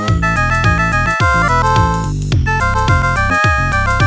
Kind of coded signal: none